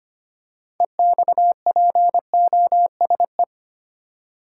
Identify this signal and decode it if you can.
Morse code, heard as a keyed tone, about 25 wpm, 700 Hz. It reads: EXPOSE
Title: Six Letter Words - 25wpm